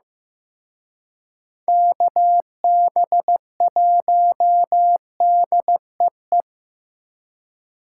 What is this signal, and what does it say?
Morse code, heard as a keyed tone, about 18 wpm, 700 Hz. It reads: KB1DEE
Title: Speed Racing - USA 2x3 Call Signs 15wpm